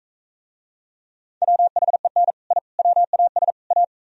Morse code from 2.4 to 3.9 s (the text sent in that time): I WAS A